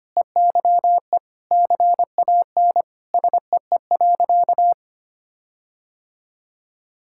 Morse code, heard as a keyed tone, about 25 wpm, 700 Hz. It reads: EYE CAN SEE.
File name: practice / 25wpm / Sentences with 4-Word Phrases - 25wpm